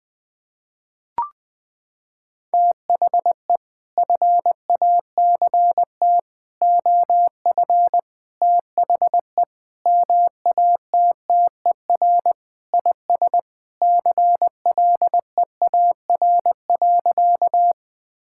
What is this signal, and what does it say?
Morse code, heard as a keyed tone, about 19 wpm, 700 Hz. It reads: THE FACT OF THE MATTER IS CLEAR.
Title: Sentences with 6-Word Phrases - 20wpm